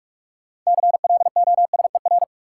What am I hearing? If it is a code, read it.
Morse code, heard as a keyed tone, about 46 wpm, 700 Hz. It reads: CLOSER
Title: Sets of 5 Words Encore - New Word Drill - 45wpm